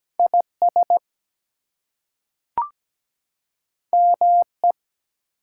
Morse code, heard as a keyed tone, about 17 wpm, 700 Hz. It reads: IS ME
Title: Two Letter Words - 17wpm